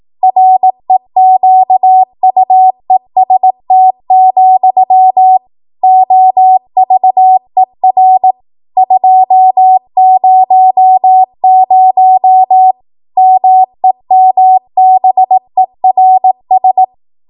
Morse code, heard as a keyed tone, about 18 wpm, 750 Hz. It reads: REQUEST, OVER 200 MEMBERS
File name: Morse / 18WPM